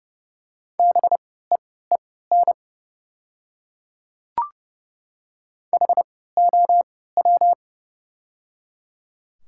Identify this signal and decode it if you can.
Morse code, heard as a keyed tone, about 30 wpm, spaced as for 17 wpm, 700 Hz. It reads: BEEN HOW
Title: 3x Character Spacing - Top 300 Words - 30wpm